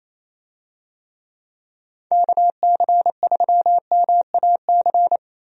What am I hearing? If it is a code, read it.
Morse code, heard as a keyed tone, about 27 wpm, 700 Hz. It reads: KC3MAC
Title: USA Call Signs - 6 Characters - 28wpm